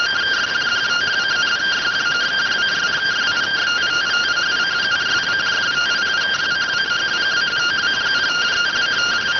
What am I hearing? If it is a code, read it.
Two-tone teleprinter signal, about 170 Hz shift.